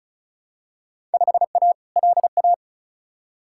Morse code, heard as a keyed tone, about 36 wpm, 700 Hz. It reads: 5A LA
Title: Field Day Exchanges Encore - 35wpm